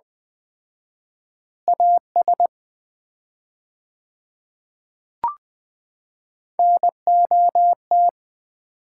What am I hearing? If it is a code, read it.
Morse code, heard as a keyed tone, about 20 wpm, 700 Hz. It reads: AS NOT
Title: 4x Word-Spacing - Sets of 2 Words - 20wpm